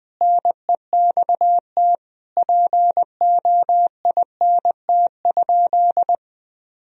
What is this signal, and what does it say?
Morse code, heard as a keyed tone, about 20 wpm, 700 Hz. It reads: NEXT POINT?
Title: Sentences from Top 1000 Words - Review Part 1 - 20wpm